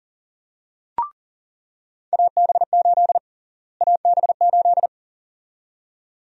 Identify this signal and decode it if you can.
Morse code, heard as a keyed tone, about 40 wpm, 700 Hz. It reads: AB8 AB8
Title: USA Call Sign Prefixes - Repeated 2x - 40wpm